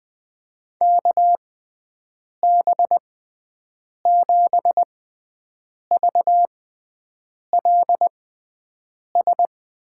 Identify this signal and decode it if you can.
Morse code, heard as a keyed tone, about 20 wpm, spaced as for 7 wpm, 700 Hz. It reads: KB7VLS
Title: USA Call Signs - 6 Characters - 6x Character Spacing - 20wpm